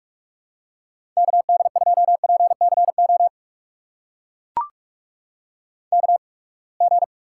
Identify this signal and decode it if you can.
Morse code, heard as a keyed tone, about 44 wpm, spaced as for 31 wpm, 700 Hz. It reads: KD2PCO KG
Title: USA Call Signs - 8x Character Spacing - 45wpm